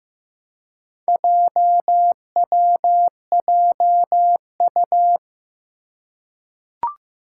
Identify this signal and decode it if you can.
Morse code, heard as a keyed tone, about 15 wpm, 700 Hz. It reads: JWJU
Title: Four Challenging Letter-Number Combinations 15wpm